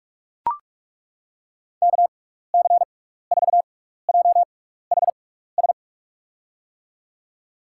Morse code, heard as a keyed tone, about 42 wpm, 700 Hz. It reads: K C 4 J H S
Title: USA Call Signs - 6 Characters - 6x Character Spacing - 45wpm